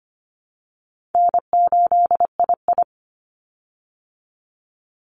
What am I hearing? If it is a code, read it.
Morse code, heard as a keyed tone, about 25 wpm, 700 Hz. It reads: N8II
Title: Call Signs Repeated 3x - 25wpm